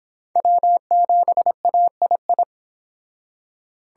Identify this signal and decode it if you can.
Morse code, heard as a keyed tone, about 26 wpm, 700 Hz. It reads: W7AII